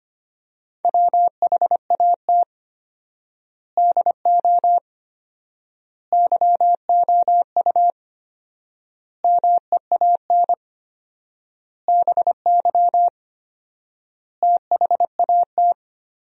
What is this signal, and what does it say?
Morse code, heard as a keyed tone, about 25 wpm, 700 Hz. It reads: WHAT DO YOU MEAN BY THAT
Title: 4x Word-Spacing - Sentences from Top 400 Words - 25wpm